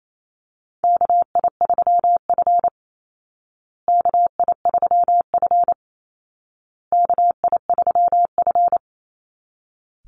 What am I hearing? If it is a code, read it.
Morse code, heard as a keyed tone, about 28 wpm, 700 Hz. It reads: KI3F KI3F KI3F
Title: Call Signs Repeated 2x - 28wpm